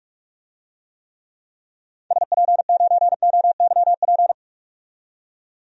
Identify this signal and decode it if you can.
Morse code, heard as a keyed tone, about 42 wpm, 700 Hz. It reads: SP9OYP
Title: DXCC-100 Call Signs - 6 Characters - 45wpm